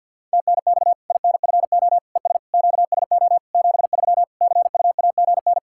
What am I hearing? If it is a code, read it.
Morse code, heard as a keyed tone, about 52 wpm, 700 Hz. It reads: TNX INFO ES QSO 73 CUAGN
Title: Practice QSO - Exchange 4 - Ending - 50wpm